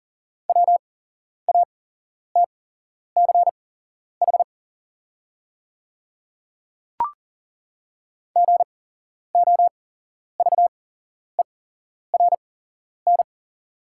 Morse code, heard as a keyed tone, about 40 wpm, spaced as for 11 wpm, 690 Hz. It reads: WATCH GOVERN
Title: 8x Character Spacing - Top 500 Words - 40wpm